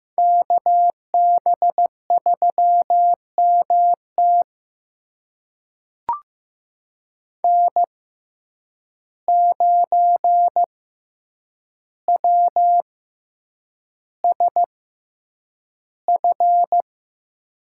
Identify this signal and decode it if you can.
Morse code, heard as a keyed tone, about 15 wpm, spaced as for 8 wpm, 700 Hz. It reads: KB3MT N9WSF
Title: USA Call Signs - 5 Characters - 6x Character Spacing - 15wpm